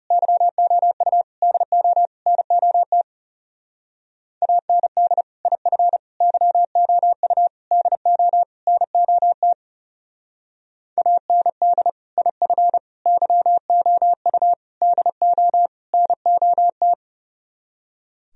Morse code, heard as a keyed tone, about 37 wpm, 700 Hz. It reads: YOU DO NOT AND IF YOU DO NOT AND IF YOU DO NOT
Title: Speed-Racing - Sets of 5 Words 30wpm